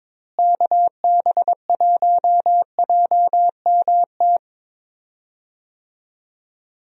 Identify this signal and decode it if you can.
Morse code, heard as a keyed tone, about 22 wpm, 700 Hz. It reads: KB1JMT
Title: USA Call Signs - 6 Characters - Repeated 3x - 22wpm